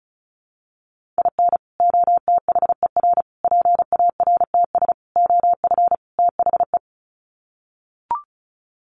Morse code, heard as a keyed tone, about 34 wpm, 700 Hz. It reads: IN OTHER PARTS OF THE
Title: Sets of 5 Words 35wpm